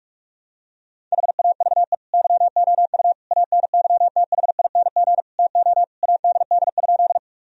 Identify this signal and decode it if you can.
Morse code, heard as a keyed tone, about 43 wpm, 700 Hz. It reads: HAVE YOU ANYTHING TO ADD?
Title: Sentences from Top 1000 Words - Review Part 1 - 45wpm